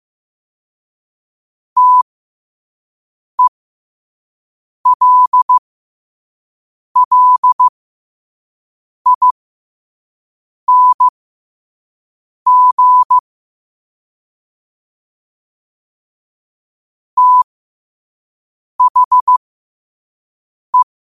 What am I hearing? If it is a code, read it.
Morse code, heard as a keyed tone, about 15 wpm, spaced as for 5 wpm, 1000 Hz. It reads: TELLING THE
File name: qotdfiles / 5wpm